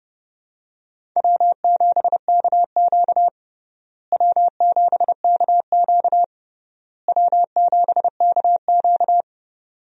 Morse code, heard as a keyed tone, about 30 wpm, 700 Hz. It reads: W7KQ W7KQ W7KQ
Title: USA Call Signs - 4 Characters - Repeated 3x - 30wpm